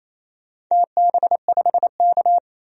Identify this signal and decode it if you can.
Morse code, heard as a keyed tone, about 28 wpm, 700 Hz. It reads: TB5K
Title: DXCC-100 Call Signs - 4 Characters - 28wpm